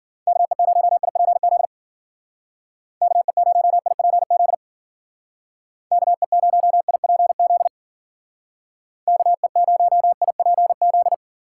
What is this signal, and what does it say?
Morse code, heard as a keyed tone, about 49 wpm, 700 Hz. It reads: KE0IPZ KE0IPZ KE0IPZ KE0IPZ
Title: Speed Racing - USA 2x3 Call Signs 40wpm